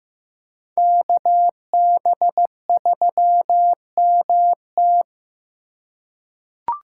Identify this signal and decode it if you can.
Morse code, heard as a keyed tone, about 15 wpm, 700 Hz. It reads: KB3MT
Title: USA Call Signs - 5 Characters - 4x Character Spacing - 15wpm